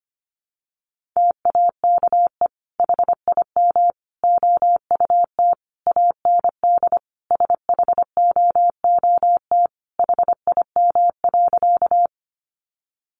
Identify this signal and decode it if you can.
Morse code, heard as a keyed tone, about 25 wpm, 700 Hz. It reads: TAKE HIM OUT AND SHOOT HIM.
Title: Sentences from Top 1200 Words 25wpm